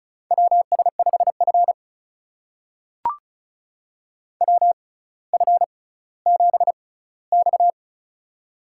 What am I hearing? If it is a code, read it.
Morse code, heard as a keyed tone, about 36 wpm, spaced as for 18 wpm, 700 Hz. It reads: WS5F WF7X
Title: USA Call Signs - 4 Characters - 6x Character Spacing - 35wpm